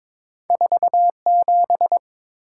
Morse code, heard as a keyed tone, about 23 wpm, 700 Hz. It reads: 47
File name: practice / 22wpm / Two Letter-Number Combinations - 22wpm